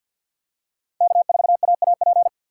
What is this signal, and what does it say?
Morse code, heard as a keyed tone, about 48 wpm, 700 Hz. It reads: K4AAP
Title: USA Call Signs - 4x Character Spacing - 50wpm